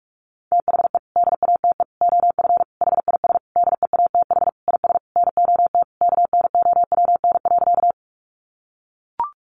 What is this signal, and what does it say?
Morse code, heard as a keyed tone, about 44 wpm, 700 Hz. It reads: THE DATE OF HIS DEATH IS NOT KNOWN.
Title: Sentences from Top 1100 Words 45wpm